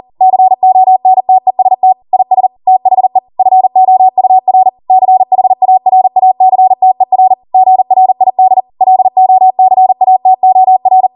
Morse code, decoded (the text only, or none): CONTEST IS THE FOUR CHARACTER GRID LOCATOR